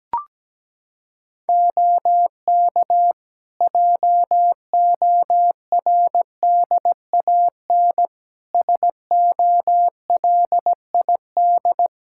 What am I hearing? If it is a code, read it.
Morse code, heard as a keyed tone, about 17 wpm, 700 Hz. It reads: OK JORDAN SOLID
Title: Practice QSO - Exchange 3 - More Info - 17wpm